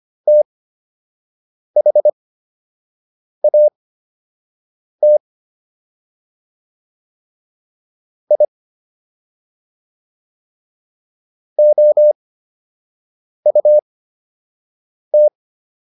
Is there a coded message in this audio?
Yes: Morse code from 0.3 to 15.3 s, heard as a keyed tone, about 25 wpm, spaced as for 6 wpm, 600 Hz. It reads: THAT I OUT